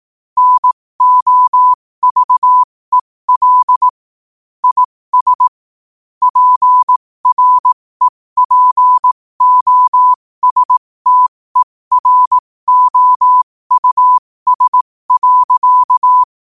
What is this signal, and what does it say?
Morse code, heard as a keyed tone, about 18 wpm, spaced as for 15 wpm, 1000 Hz. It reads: NOVEL IS PREPOSTEROUS.
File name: qotdfiles / 15wpm